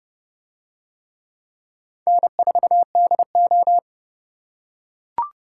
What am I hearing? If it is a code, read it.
Morse code, heard as a keyed tone, about 27 wpm, 700 Hz. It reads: N4DO